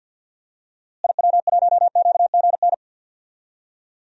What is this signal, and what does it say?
Morse code, heard as a keyed tone, about 47 wpm, 700 Hz. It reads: IW1QGN
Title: DXCC-100 Call Signs - 6 Characters - Repeated 2x - 50wpm